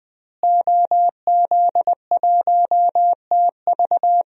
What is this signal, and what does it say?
Morse code, heard as a keyed tone, about 20 wpm, 700 Hz. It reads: OZ1TV
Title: DXCC-100 Call Signs - 5 Characters - Repeated 3x - 20wpm